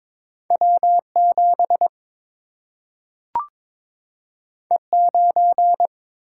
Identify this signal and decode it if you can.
Morse code, heard as a keyed tone, about 22 wpm, 700 Hz. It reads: W 7 E 9
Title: Single-Letter Single-Number 22wpm